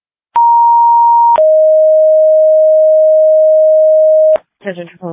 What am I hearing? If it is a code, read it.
Two-tone page, heard as a one-second tone, then a three-second tone, about 940 then 620 Hz, with speech. It is heard from 0.4 to 4.4 s.